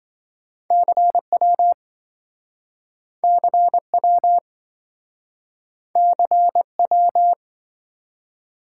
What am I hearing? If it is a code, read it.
Morse code, heard as a keyed tone, about 27 wpm, 700 Hz. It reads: CW CW CW